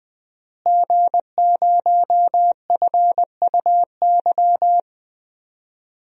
Morse code, heard as a keyed tone, about 21 wpm, 700 Hz. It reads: G0FUY
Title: DXCC-100 Call Signs - 5 Characters - Repeated 2x - 20wpm